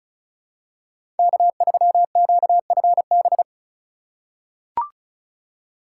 Morse code, heard as a keyed tone, about 36 wpm, 700 Hz. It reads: K3QFB
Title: USA Call Signs - 5 Characters - 4x Character Spacing - 35wpm